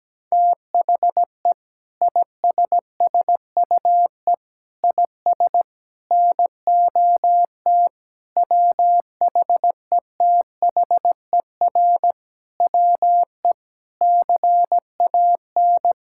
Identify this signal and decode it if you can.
Morse code, heard as a keyed tone, about 17 wpm, 700 Hz. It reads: THE ISSUE IS NOT WHETHER WE CAN
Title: Sentences from Top 1000 Words - Review Part 6 - 17wpm